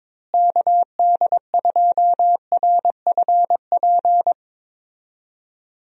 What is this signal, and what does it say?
Morse code, heard as a keyed tone, about 22 wpm, 700 Hz. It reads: KD2RFP
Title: USA Call Signs - 6 Characters - Repeated 3x - 22wpm